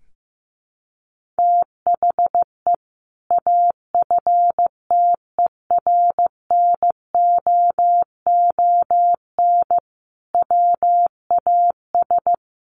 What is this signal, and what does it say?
Morse code, heard as a keyed tone, about 15 wpm, 700 Hz. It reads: THE AFTERNOON WAS
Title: Sentences from Top 1500 Words 15wpm